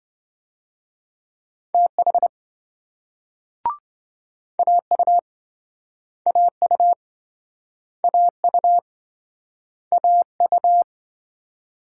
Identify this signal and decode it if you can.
Morse code, heard as a keyed tone, about 28 wpm, 700 Hz. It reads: TH AU AU AU AU